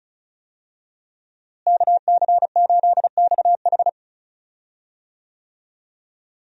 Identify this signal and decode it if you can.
Morse code, heard as a keyed tone, about 34 wpm, 700 Hz. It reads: KC8XH
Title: USA Call Signs - 5 Characters - Repeated 3x - 35wpm